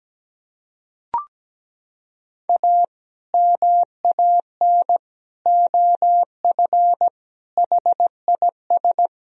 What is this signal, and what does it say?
Morse code, heard as a keyed tone, about 17 wpm, 700 Hz. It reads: A MAN OF HIS
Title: Sentences from Top 1000 Words - Review Part 3 - 17wpm